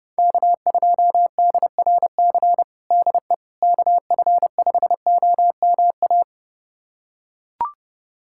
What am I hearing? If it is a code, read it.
Morse code, heard as a keyed tone, about 30 wpm, 700 Hz. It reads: K2DRC DE KF5OMA
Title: USA Callsign DE Callsign 30wpm